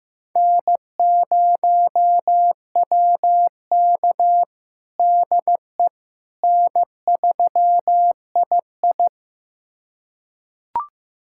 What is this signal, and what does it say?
Morse code, heard as a keyed tone, about 15 wpm, 700 Hz. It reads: N0WK DE N3II
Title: USA Callsign DE Callsign - 4 Characters - 15wpm